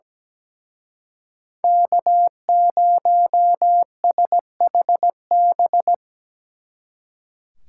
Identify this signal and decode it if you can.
Morse code, heard as a keyed tone, about 20 wpm, 700 Hz. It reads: K0SHB
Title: Speed Racing - USA 1x3 Call Signs 17wpm